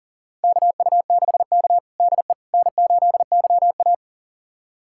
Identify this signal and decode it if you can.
Morse code, heard as a keyed tone, about 37 wpm, 700 Hz. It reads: KU6K DE N8YA
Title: USA Callsign DE Callsign - 4 Characters - 40wpm